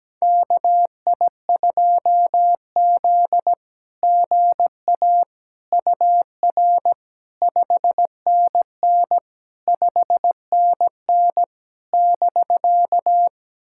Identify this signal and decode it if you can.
Morse code, heard as a keyed tone, about 17 wpm, 700 Hz. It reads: KI2Z GA UR 5NN 5NN <BK>
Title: SOTA QSO Practice 17wpm